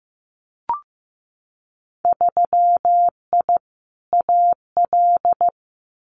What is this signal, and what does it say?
Morse code, heard as a keyed tone, about 15 wpm, 700 Hz. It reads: 3I AL
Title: Winter Field Day Exchanges 15wpm